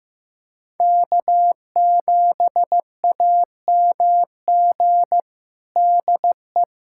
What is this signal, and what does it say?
Morse code, heard as a keyed tone, about 15 wpm, 700 Hz. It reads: K7AMG DE
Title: USA Callsign DE Callsign 15wpm